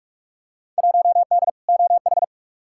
Morse code, heard as a keyed tone, about 46 wpm, 700 Hz. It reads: 1D OH
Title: Field Day Exchanges - 45wpm